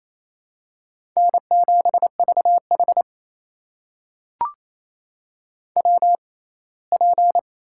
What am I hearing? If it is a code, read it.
Morse code, heard as a keyed tone, about 28 wpm, spaced as for 19 wpm, 700 Hz. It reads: N7VH WP
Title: USA Call Signs - 6x Character Spacing - 28wpm